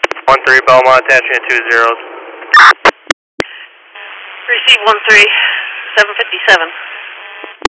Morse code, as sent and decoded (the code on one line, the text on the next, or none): none